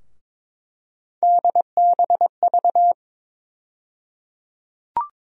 Morse code, heard as a keyed tone, about 21 wpm, 700 Hz. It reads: DBV
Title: USA Call Sign Postfixes - 3 Letters 22wpm